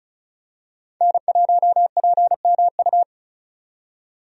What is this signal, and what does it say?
Morse code, heard as a keyed tone, about 34 wpm, 700 Hz. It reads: N1PMU
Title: USA Call Signs - 5 Characters - 35wpm